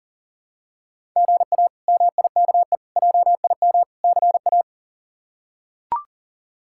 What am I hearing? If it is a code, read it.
Morse code, heard as a keyed tone, about 37 wpm, 700 Hz. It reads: GA MIKE JIM CA
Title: K1USN SST QSO Practice - 40wpm